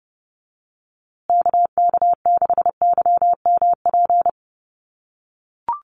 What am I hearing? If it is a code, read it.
Morse code, heard as a keyed tone, about 30 wpm, 700 Hz. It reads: KK6YMP
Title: USA Call Signs Repeated 3x 30wpm